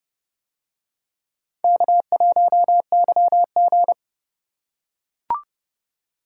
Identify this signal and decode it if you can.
Morse code, heard as a keyed tone, about 29 wpm, 700 Hz. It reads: K1YG